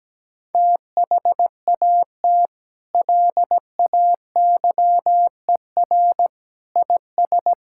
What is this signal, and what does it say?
Morse code, heard as a keyed tone, about 17 wpm, 700 Hz. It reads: THAT LAYER IS
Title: Sentences from Top 1600 Words 17wpm